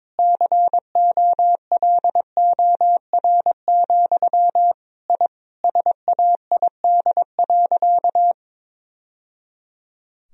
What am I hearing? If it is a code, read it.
Morse code, heard as a keyed tone, about 22 wpm, 700 Hz. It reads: COLOR, I SAID.
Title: Sentences from Top 1000 Words - Review Part 2 - 22wpm